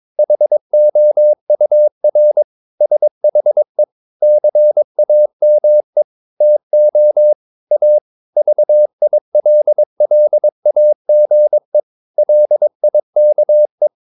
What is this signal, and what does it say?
Morse code, heard as a keyed tone, about 22 wpm, 600 Hz. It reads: HOUR SHE CAME TO A VILLAGE LIKE
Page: Romanen Never af Ken Follet på engelsk.